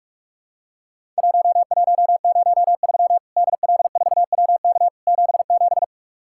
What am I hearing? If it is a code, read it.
Morse code, heard as a keyed tone, about 44 wpm, 700 Hz. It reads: 1103 DL4WK 77
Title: Worked All Europe DX Contest - QTC Exchange - 45wpm